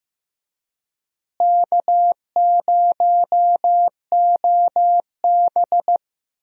Morse code, heard as a keyed tone, about 15 wpm, 700 Hz. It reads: K0OB
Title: USA Call Signs - 4 Characters - Repeated 3x - 15wpm